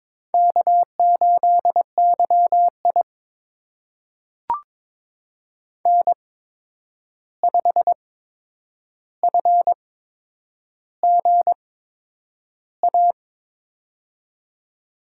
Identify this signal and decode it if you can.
Morse code, heard as a keyed tone, about 22 wpm, spaced as for 9 wpm, 700 Hz. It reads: K8YI N5FGA
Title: USA Call Signs - 8x Character Spacing - 22wpm